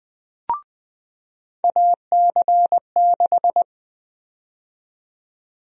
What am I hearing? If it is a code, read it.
Morse code, heard as a keyed tone, about 20 wpm, 700 Hz. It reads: AC6
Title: USA Call Sign Prefixes - 20wpm